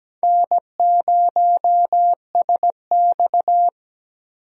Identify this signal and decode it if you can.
Morse code, heard as a keyed tone, about 17 wpm, 700 Hz. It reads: N0SX
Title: USA Call Signs - 4 Characters - Repeated 3x - 17wpm